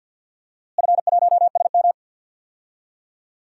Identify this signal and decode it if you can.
Morse code, heard as a keyed tone, about 49 wpm, 700 Hz. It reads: F1SM
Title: DXCC-100 Call Signs - 4 Characters - Repeated 3x - 50wpm